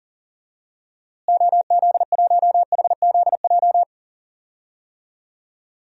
Morse code, heard as a keyed tone, about 39 wpm, 700 Hz. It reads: OZ1HZJ
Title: DXCC-100 Call Signs - 6 Characters - Repeated 3x - 40wpm